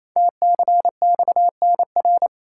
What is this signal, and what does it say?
Morse code, heard as a keyed tone, about 28 wpm, 700 Hz. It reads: TCXNR
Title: Code Groups - 5 Letters - 28wpm